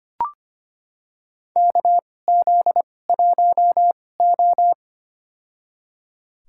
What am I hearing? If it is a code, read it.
Morse code, heard as a keyed tone, about 25 wpm, spaced as for 18 wpm, 700 Hz. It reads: KZ1O
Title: USA Call Signs - 4 Characters - 2x Character Spacing - 25wpm